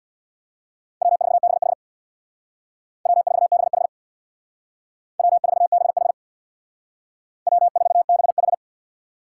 Morse code, heard as a keyed tone, about 62 wpm, 700 Hz. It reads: W4BH W4BH W4BH W4BH
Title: Speed Racing - USA 1x2 Call Signs 50wpm